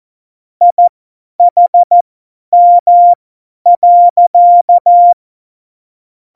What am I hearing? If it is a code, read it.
Morse code, heard as a keyed tone, about 14 wpm, spaced as for 10 wpm, 700 Hz. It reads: IHM.
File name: Morsekurs / Witze / Witz 27. Oktober 2020, 50 F70